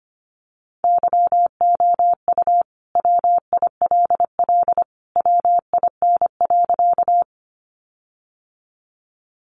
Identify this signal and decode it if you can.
Morse code, heard as a keyed tone, about 25 wpm, 700 Hz. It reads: YOU WILL WIN.
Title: Sentences from Top 1500 Words 25wpm